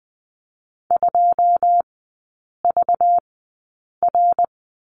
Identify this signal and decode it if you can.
Morse code, heard as a keyed tone, about 20 wpm, spaced as for 8 wpm, 700 Hz. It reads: 2VR
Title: 191 Lesson 31e - Letter v - Three Characters 20wpm